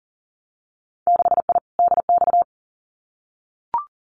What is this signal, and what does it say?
Morse code, heard as a keyed tone, about 40 wpm, 700 Hz. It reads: 6I DX
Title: Winter Field Day Exchanges Encore 40wpm